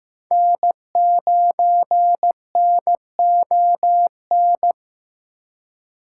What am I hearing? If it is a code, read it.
Morse code, heard as a keyed tone, about 15 wpm, 700 Hz. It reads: N9NON
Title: USA Call Signs - 3x Character Spacing - 15wpm